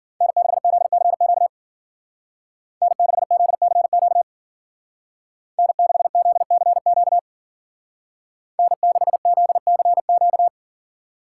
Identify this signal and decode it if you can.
Morse code, heard as a keyed tone, about 49 wpm, 700 Hz. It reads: N6ZCQ N6ZCQ N6ZCQ N6ZCQ
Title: Speed Racing - USA Call Signs 40wpm